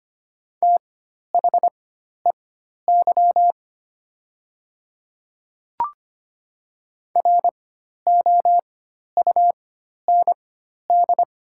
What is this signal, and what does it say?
Morse code, heard as a keyed tone, about 25 wpm, spaced as for 12 wpm, 700 Hz. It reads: THEY ROUND
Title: 4x Character Spacing - Top 200 Words - 25wpm